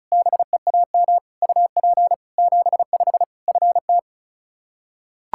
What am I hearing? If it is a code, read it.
Morse code, heard as a keyed tone, about 36 wpm, 700 Hz. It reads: BEAM UP 75 FT
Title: Practice QSO - Exchange 2 - Info - 35wpm